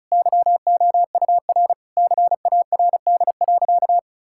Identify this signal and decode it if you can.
Morse code, heard as a keyed tone, about 34 wpm, 700 Hz. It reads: YOUR CARD.
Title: Sentences from Top 1000 Words - Review Part 6 - 35wpm